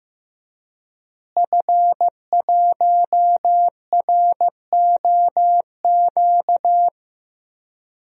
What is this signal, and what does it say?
Morse code, heard as a keyed tone, about 15 wpm, 700 Hz. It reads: F1ROQ
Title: DXCC-100 Call Signs - 5 Characters - Repeated 3x - 15wpm